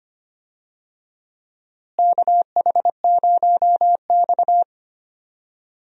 Morse code, heard as a keyed tone, about 25 wpm, 700 Hz. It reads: KH0X
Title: USA Call Signs - 4 Characters - Repeated 2x - 25wpm